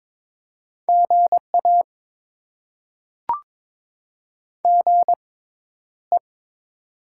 Morse code, heard as a keyed tone, about 22 wpm, spaced as for 11 wpm, 700 Hz. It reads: GA GE
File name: Top 20 QSO Elements - 6x Character Spacing - 22wpm